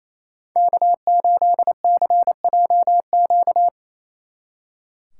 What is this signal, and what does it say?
Morse code, heard as a keyed tone, about 33 wpm, 700 Hz. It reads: K8CJQ
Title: Speed Racing - USA Call Signs 28wpm